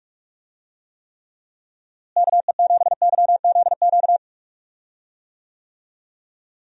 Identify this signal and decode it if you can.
Morse code, heard as a keyed tone, about 42 wpm, 700 Hz. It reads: KE7YZQ
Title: USA Call Signs - 6 Characters - Repeated 2x - 45wpm